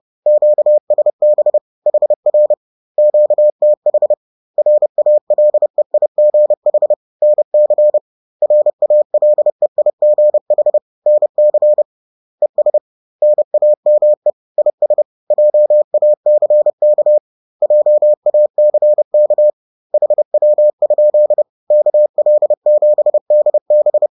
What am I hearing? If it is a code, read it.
Morse code, heard as a keyed tone, about 30 wpm, 600 Hz. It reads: QSB HR QTH RALEIGH NC RALEIGH NC ES NAME IS JACK JACK HW? KL7DB
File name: 30wpm